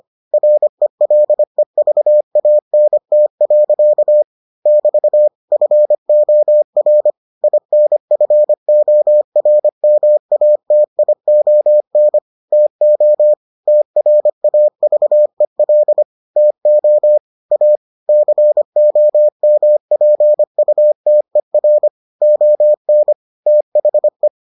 Morse code, heard as a keyed tone, about 25 wpm, 600 Hz. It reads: RELEVANT. <BT> FOR INFORMATION TO TRAVEL TO A COMPUTER ON THE